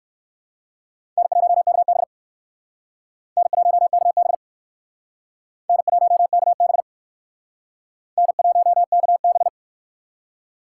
Morse code, heard as a keyed tone, about 59 wpm, 700 Hz. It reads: N1KB N1KB N1KB N1KB